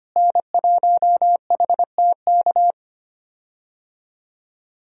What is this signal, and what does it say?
Morse code, heard as a keyed tone, about 25 wpm, 700 Hz. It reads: N1HTK
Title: USA Call Signs - 5 Characters - Repeated 3x - 25wpm